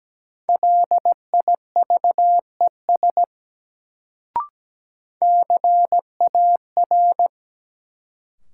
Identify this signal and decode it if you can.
Morse code, heard as a keyed tone, about 17 wpm, 700 Hz. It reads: LIVES CAR
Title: Sets of 5 Words Encore - New Word Drill - 17wpm